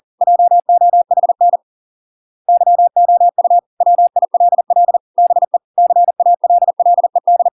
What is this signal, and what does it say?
Morse code, heard as a keyed tone, about 40 wpm, 700 Hz. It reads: JOHN YOU WILL BE CALLED